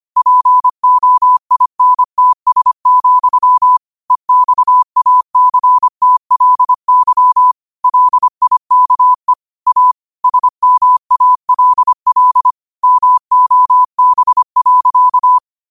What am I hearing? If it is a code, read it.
Morse code, heard as a keyed tone, about 25 wpm, 1000 Hz. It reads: POINTS, EXACTLY LIKE A SMALL MOB.
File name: qotdfiles / 25wpm